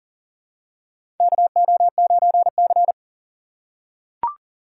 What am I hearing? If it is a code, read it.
Morse code, heard as a keyed tone, about 37 wpm, 700 Hz. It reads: KO9C
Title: USA Call Signs - 4 Characters - 6x Character Spacing - 40wpm